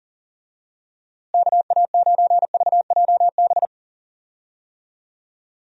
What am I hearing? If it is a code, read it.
Morse code, heard as a keyed tone, about 40 wpm, 700 Hz. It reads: KA9VJB
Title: USA Call Signs - 6 Characters - Repeated 2x - 40wpm